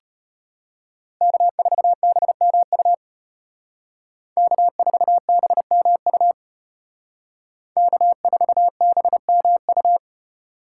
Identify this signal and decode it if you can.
Morse code, heard as a keyed tone, about 32 wpm, 700 Hz. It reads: K4BMU K4BMU K4BMU